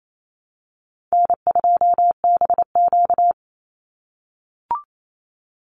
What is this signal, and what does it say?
Morse code, heard as a keyed tone, about 28 wpm, 700 Hz. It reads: N2BQ
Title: USA Call Signs 28wpm